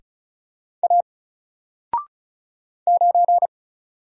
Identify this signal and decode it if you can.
Morse code, heard as a keyed tone, about 36 wpm, 700 Hz. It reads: A 9
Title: Single Character (Letter:Number:Punctuation:Prosign) Alt 35wpm